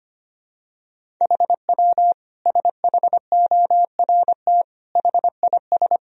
Morse code, heard as a keyed tone, about 25 wpm, 700 Hz. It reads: HW SHORT HIS
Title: Sentences from Top 400 Words and Abbreviations 25wpm